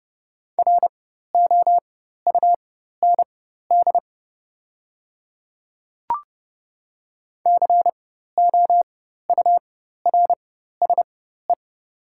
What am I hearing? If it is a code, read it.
Morse code, heard as a keyed tone, about 30 wpm, spaced as for 14 wpm, 690 Hz. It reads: ROUND COURSE
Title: 4x Character Spacing - Top 500 Words - 30wpm